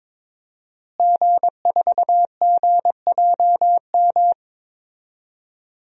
Morse code, heard as a keyed tone, about 22 wpm, 700 Hz. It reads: G4GJM
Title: DXCC-100 Call Signs - 5 Characters - Repeated 2x - 22wpm